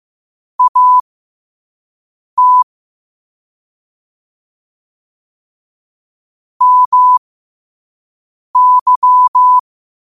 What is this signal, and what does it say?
Morse code, heard as a keyed tone, about 15 wpm, spaced as for 5 wpm, 1000 Hz. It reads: AT MY
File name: qotdfiles / 5wpm